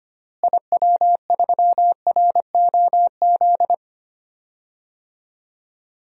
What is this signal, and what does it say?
Morse code, heard as a keyed tone, about 25 wpm, 700 Hz. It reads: IW3ROZ
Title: DXCC-100 Call Signs - 6 Characters - Repeated 2x - 25wpm